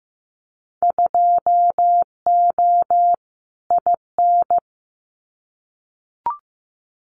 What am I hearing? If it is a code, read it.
Morse code, heard as a keyed tone, about 15 wpm, 700 Hz. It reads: 2O IN
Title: Winter Field Day Exchanges 15wpm